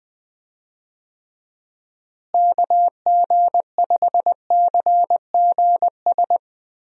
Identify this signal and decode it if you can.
Morse code, heard as a keyed tone, about 20 wpm, 700 Hz. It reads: KG5CGS
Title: USA Call Signs - 6 Characters - Repeated 3x - 20wpm